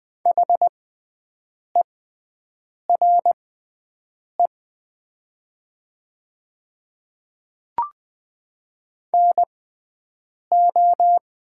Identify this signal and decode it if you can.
Morse code, heard as a keyed tone, about 20 wpm, spaced as for 7 wpm, 700 Hz. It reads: HERE NO
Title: 6x Character Spacing - Top 200 Words - 20wpm